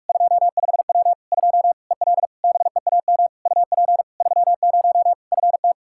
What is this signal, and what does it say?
Morse code, heard as a keyed tone, about 41 wpm, 700 Hz. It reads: 25W 2 EL BEAM UP 30 FT
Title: Practice QSO - Exchange 2 - Info - 45wpm